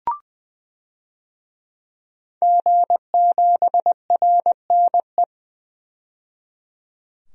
Morse code, heard as a keyed tone, about 20 wpm, 700 Hz. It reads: G7RNE
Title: DXCC-100 Call Signs - 5 Characters - 20wpm